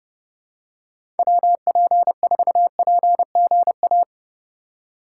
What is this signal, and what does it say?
Morse code, heard as a keyed tone, about 30 wpm, 700 Hz. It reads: WP4PGA
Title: USA Call Signs - 6 Characters - 4x Character Spacing - 30wpm